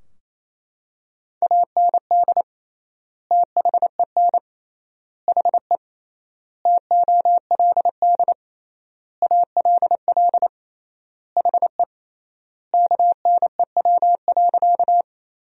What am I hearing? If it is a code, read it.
Morse code, heard as a keyed tone, about 28 wpm, 700 Hz. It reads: AND THEN HE TOLD ALL HE KNEW.
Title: 3x Word-Spacing - Sentences from Top 400 Words - 28wpm